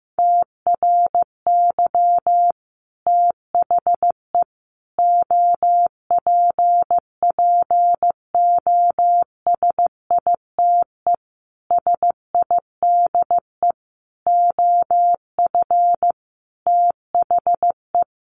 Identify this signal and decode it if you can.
Morse code, heard as a keyed tone, about 15 wpm, 700 Hz. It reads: TRY THE OPPOSITE SIDE OF THE
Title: Sentences from Top 1300 Words 15wpm